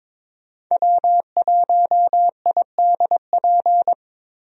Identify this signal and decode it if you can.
Morse code, heard as a keyed tone, about 22 wpm, 700 Hz. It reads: W1IDP
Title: USA Call Signs - 5 Characters - 22wpm